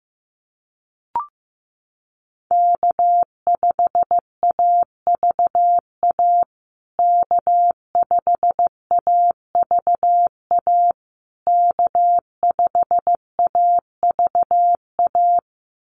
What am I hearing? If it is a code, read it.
Morse code, heard as a keyed tone, about 15 wpm, 700 Hz. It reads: K5AVA K5AVA K5AVA
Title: USA Call Signs Repeated 3x 15wpm